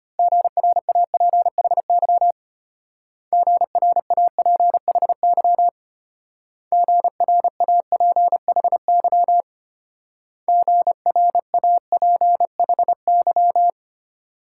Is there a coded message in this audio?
Yes: Morse code from 0.2 to 13.7 s, heard as a keyed tone, about 32 wpm, 700 Hz. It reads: GRAPHY GRAPHY GRAPHY GRAPHY